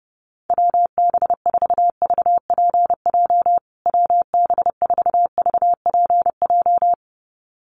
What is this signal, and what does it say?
Morse code, heard as a keyed tone, about 28 wpm, 700 Hz. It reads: WB4VPJ WB4VPJ
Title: USA Call Signs Repeated 2x 30wpm